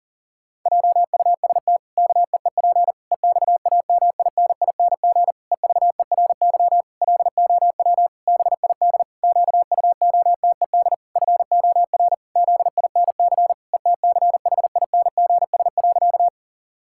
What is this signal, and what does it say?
Morse code, heard as a keyed tone, about 37 wpm, 700 Hz. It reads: JUST KEEP EXAMINING EVERY LOW BID QUOTED FOR ZINC ETCHINGS.
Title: Pangram Challenge v2 - 40wpm